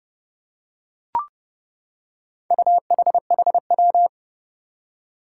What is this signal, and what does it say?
Morse code, heard as a keyed tone, about 30 wpm, 700 Hz. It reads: UHHW
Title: Four Challenging Letter-Number Combinations 30wpm